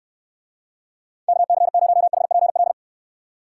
Morse code, heard as a keyed tone, about 69 wpm, 700 Hz. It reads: KC0HPL